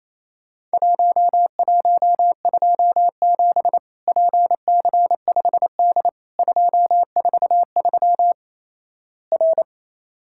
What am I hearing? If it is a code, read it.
Morse code, heard as a keyed tone, about 28 wpm, 700 Hz. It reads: 1127 PC5D 243 R
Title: Worked All Europe DX Contest - QTC Exchange - 28wpm